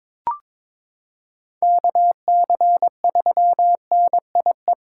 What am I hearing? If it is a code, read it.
Morse code, heard as a keyed tone, about 22 wpm, 700 Hz. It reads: KC3NIE
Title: K1USN SST QSO Practice - 22wpm